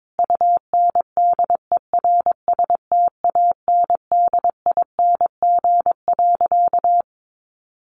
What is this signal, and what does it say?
Morse code, heard as a keyed tone, about 22 wpm, 700 Hz. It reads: UNDERSTANDING.
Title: Sentences from Top 1400 Words 22wpm